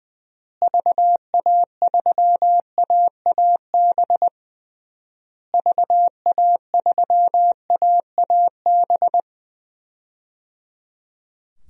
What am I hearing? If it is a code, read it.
Morse code, heard as a keyed tone, about 20 wpm, 700 Hz. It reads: VA3AAB VA3AAB
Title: DXCC-100 Call Signs - 6 Characters - Repeated 3x - 20wpm